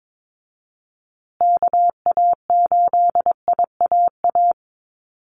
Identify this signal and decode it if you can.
Morse code, heard as a keyed tone, about 22 wpm, 700 Hz. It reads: KA8IAA
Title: USA Call Signs Repeated 3x 22wpm